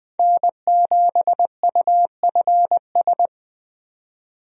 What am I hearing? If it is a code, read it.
Morse code, heard as a keyed tone, about 20 wpm, 700 Hz. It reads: N7UFS
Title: USA Call Signs - 5 Characters - 20wpm